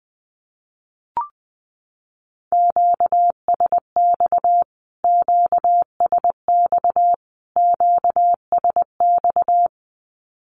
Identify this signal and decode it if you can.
Morse code, heard as a keyed tone, about 20 wpm, 700 Hz. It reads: QSX QSX QSX
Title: Q-Codes Repeated 3x 20wpm